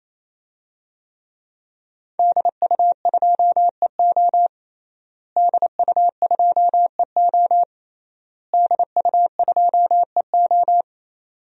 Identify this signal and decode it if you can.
Morse code, heard as a keyed tone, about 28 wpm, 700 Hz. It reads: DU2EO DU2EO DU2EO
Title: DXCC-100 Call Signs - 5 Characters - Repeated 3x - 28wpm